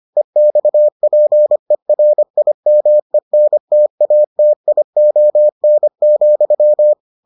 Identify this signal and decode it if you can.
Morse code, heard as a keyed tone, about 25 wpm, 600 Hz. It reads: EXPERIMENTATION,